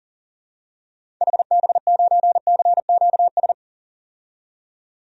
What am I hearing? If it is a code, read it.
Morse code, heard as a keyed tone, about 40 wpm, 700 Hz. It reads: HB9CQS
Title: DXCC-100 Call Signs - 6 Characters - 40wpm